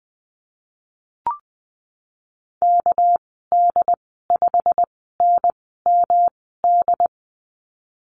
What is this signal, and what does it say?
Morse code, heard as a keyed tone, about 20 wpm, spaced as for 15 wpm, 700 Hz. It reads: KD5NMD
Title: 231 Lesson 37g - Letter x - Call Signs 20wpm